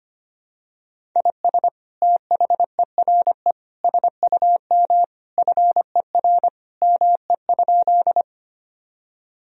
Morse code, heard as a keyed tone, about 25 wpm, 700 Hz. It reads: IS THERE SUM FER ME?
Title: Sentences from Top 200 Words and Abbreviations 25wpm